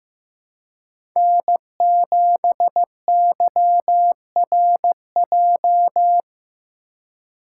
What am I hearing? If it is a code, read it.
Morse code, heard as a keyed tone, about 15 wpm, 700 Hz. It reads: N7YRJ
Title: USA Call Signs - 5 Characters - 4x Character Spacing - 15wpm